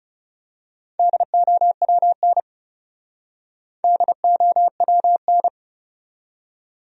Morse code, heard as a keyed tone, about 34 wpm, 700 Hz. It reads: DOWN DOWN
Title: Speed-Racing Top 100 Words 30wpm